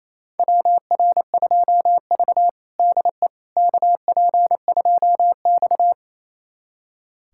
Morse code, heard as a keyed tone, about 28 wpm, 700 Hz. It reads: WR2V DE KP2X
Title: USA Callsign DE Callsign - 4 Characters - 28wpm